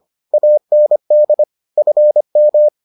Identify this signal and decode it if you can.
Morse code, heard as a keyed tone, about 25 wpm, 600 Hz. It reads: AND FM